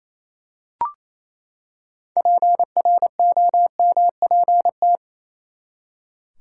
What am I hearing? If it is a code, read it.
Morse code, heard as a keyed tone, about 28 wpm, 700 Hz. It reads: PROMPT
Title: Six Letter Words - 28wpm